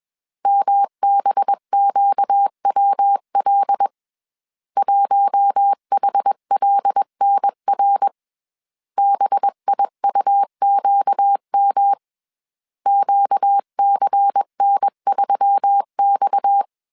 Morse code, heard as a keyed tone, about 21 wpm, 800 Hz. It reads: MBQWL 1HLNR BIUQM QCN3X
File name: CW / DET / 1 6 1500C S B 20wpm 2048